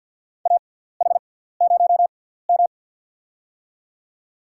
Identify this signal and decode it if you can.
Morse code, heard as a keyed tone, about 53 wpm, spaced as for 17 wpm, 700 Hz. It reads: AH0M
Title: USA Call Signs - 6x Character Spacing - 50wpm